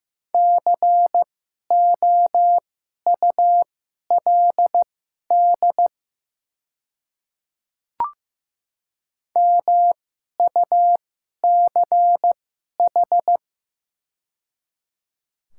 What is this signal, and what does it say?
Morse code, heard as a keyed tone, about 15 wpm, spaced as for 11 wpm, 700 Hz. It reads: COULD MUCH
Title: 2x Character Spacing - Top 300 Words - 15wpm